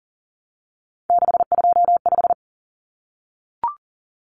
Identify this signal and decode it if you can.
Morse code, heard as a keyed tone, about 40 wpm, 700 Hz. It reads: 625
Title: Numbers - 3-Digit 40wpm